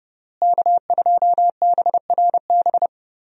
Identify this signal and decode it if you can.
Morse code, heard as a keyed tone, about 30 wpm, 700 Hz. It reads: K2BRB
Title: USA Call Signs - 5 Characters - Repeated 2x - 30wpm